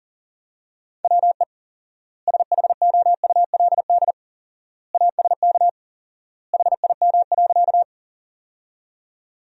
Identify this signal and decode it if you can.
Morse code, heard as a keyed tone, about 40 wpm, 700 Hz. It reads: WE SHOULD ASK HIM.
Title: 4x Word-Spacing - Sentences from Top 300 Words - 40wpm